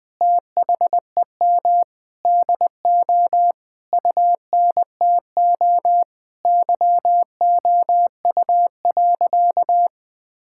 Morse code, heard as a keyed tone, about 20 wpm, 700 Hz. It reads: THEM DO UNTO YOU.
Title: Sentences with 9-Word Phrases - 20wpm